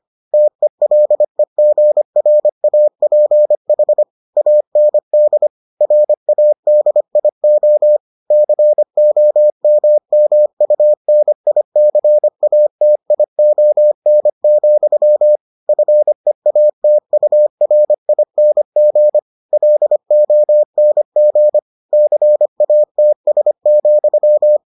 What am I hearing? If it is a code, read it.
Morse code, heard as a keyed tone, about 25 wpm, 600 Hz. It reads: TELEGRAPH AND RADIO COMMUNICATION, FEATURING LONG CATS,